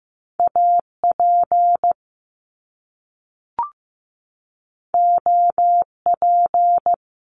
Morse code, heard as a keyed tone, about 15 wpm, 700 Hz. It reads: AP OP
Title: Top Two Letter Combinations in Words 15wpm